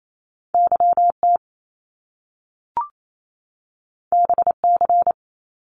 Morse code, heard as a keyed tone, about 28 wpm, 700 Hz. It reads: YT BC
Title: Canadian Provinces and Territories 28wpm